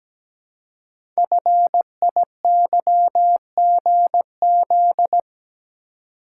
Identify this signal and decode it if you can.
Morse code, heard as a keyed tone, about 17 wpm, 700 Hz. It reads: FIYGZ
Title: Code Groups - 5 Letters - 17wpm